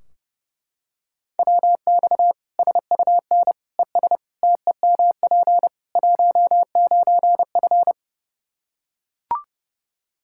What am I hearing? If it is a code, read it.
Morse code, heard as a keyed tone, about 30 wpm, 700 Hz. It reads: WX SUN ES TEMP 19F
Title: Practice QSO - Exchange 2 - Info - 30wpm